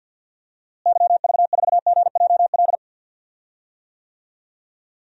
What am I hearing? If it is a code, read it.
Morse code, heard as a keyed tone, about 52 wpm, 700 Hz. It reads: YV4GJL
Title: DXCC-100 Call Signs - 6 Characters - Repeated 2x - 50wpm